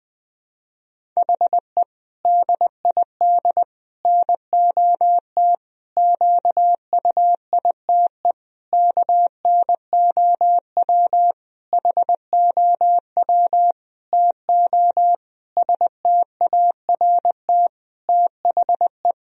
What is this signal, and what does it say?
Morse code, heard as a keyed tone, about 20 wpm, 700 Hz. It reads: HE DID NOT QUITE KNOW HOW TO START THE
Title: Sentences from Top 1000 Words - Review Part 1 - 20wpm